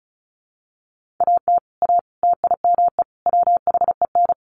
Morse code, heard as a keyed tone, about 34 wpm, 700 Hz. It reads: AT A TIME WHEN
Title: Sets of 4 Words 35wpm